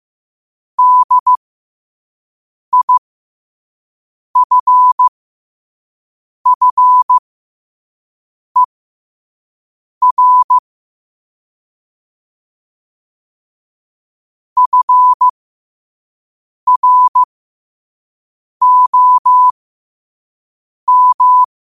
Morse code, heard as a keyed tone, about 15 wpm, spaced as for 5 wpm, 1000 Hz. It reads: DIFFER FROM
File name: qotdfiles / 5wpm